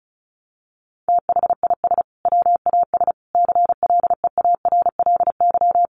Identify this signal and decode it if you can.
Morse code, heard as a keyed tone, about 34 wpm, 700 Hz. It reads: THIS WAS CLEARLY
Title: Sentences from Top 1300 Words 35wpm